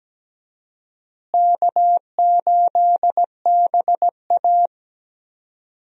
Morse code, heard as a keyed tone, about 17 wpm, 700 Hz. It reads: K8BA
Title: USA Call Signs - 4 Characters - Repeated 3x - 17wpm